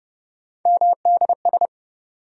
Morse code, heard as a keyed tone, about 30 wpm, 700 Hz. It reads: MDS
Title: USA Call Sign Postfixes - 3 Letters 30wpm